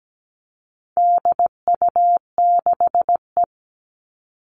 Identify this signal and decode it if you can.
Morse code, heard as a keyed tone, about 17 wpm, 700 Hz. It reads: DU6E
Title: All Cut Numbers - 4-Digit 17wpm